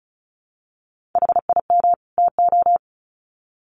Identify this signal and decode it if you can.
Morse code, heard as a keyed tone, about 34 wpm, 700 Hz. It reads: HIM TO
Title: Sets of 2 Words 35wpm